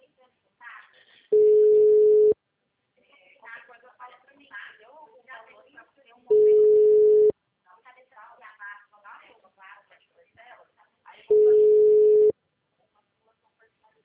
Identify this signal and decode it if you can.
Ringback tone (Europe, 425 Hz), with speech.